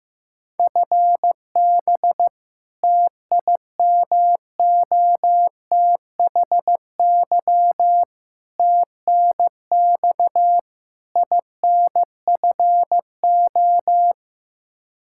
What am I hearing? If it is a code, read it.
Morse code, heard as a keyed tone, about 15 wpm, 700 Hz. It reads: FB TIMOTHY TNX INFO
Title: Practice QSO - Exchange 3 - More Info - 15wpm